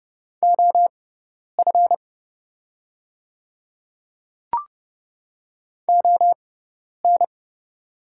Morse code, heard as a keyed tone, about 30 wpm, spaced as for 10 wpm, 700 Hz. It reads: OF ON
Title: 6x Character Spacing - Top 100 Words - 30wpm